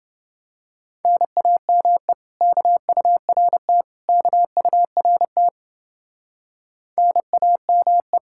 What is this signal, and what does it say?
Morse code, heard as a keyed tone, about 30 wpm, 700 Hz. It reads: NAME KURT KURT NAME